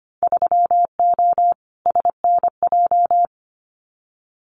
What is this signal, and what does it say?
Morse code, heard as a keyed tone, about 25 wpm, 700 Hz. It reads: 3O SNJ
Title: Winter Field Day Exchanges Encore 25wpm